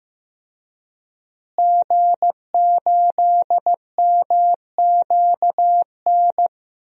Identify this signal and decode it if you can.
Morse code, heard as a keyed tone, about 15 wpm, 700 Hz. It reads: G8MQN
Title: DXCC-100 Call Signs - 5 Characters - Repeated 3x - 15wpm